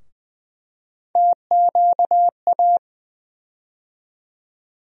Morse code, heard as a keyed tone, about 21 wpm, 700 Hz. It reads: TQA
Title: USA Call Sign Postfixes - 3 Letters 20wpm